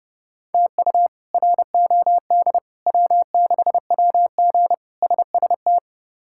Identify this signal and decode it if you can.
Morse code, heard as a keyed tone, about 32 wpm, 700 Hz. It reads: TU ROD W6WG SST
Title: K1USN SST QSO Practice - 30wpm